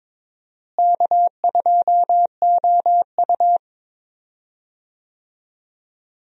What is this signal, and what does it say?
Morse code, heard as a keyed tone, about 22 wpm, 700 Hz. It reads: K2OU
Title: USA Call Signs - 4 Characters - Repeated 3x - 22wpm